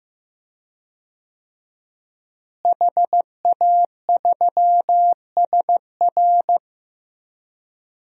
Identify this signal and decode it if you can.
Morse code, heard as a keyed tone, about 15 wpm, 700 Hz. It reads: HA3SR
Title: DXCC-100 Call Signs - 5 Characters - Repeated 3x - 15wpm